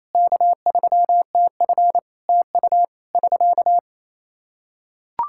Morse code, heard as a keyed tone, about 28 wpm, 700 Hz. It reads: K3TF TU <SK>
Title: Practice QSO - Exchange 4 - Ending - 28wpm